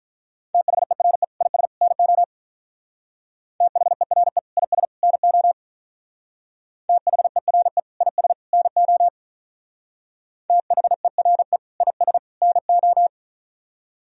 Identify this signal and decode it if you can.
Morse code, heard as a keyed tone, about 44 wpm, 700 Hz. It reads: THERE IS NO THERE IS NO THERE IS NO THERE IS NO